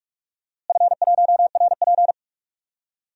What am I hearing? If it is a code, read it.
Morse code, heard as a keyed tone, about 46 wpm, 700 Hz. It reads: F1RP
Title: DXCC-100 Call Signs - 4 Characters - Repeated 3x - 45wpm